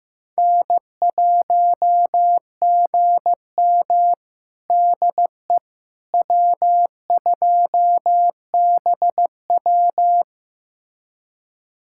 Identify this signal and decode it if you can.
Morse code, heard as a keyed tone, about 15 wpm, 700 Hz. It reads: N1GM DE W2BW
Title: USA Callsign DE Callsign - 4 Characters - 15wpm